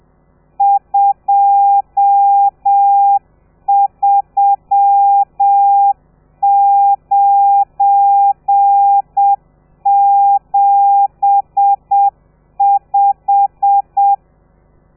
Morse code, heard as a keyed tone, about 7 wpm, 800 Hz. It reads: 23975
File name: telegrafia / i2tfj / Lez12 30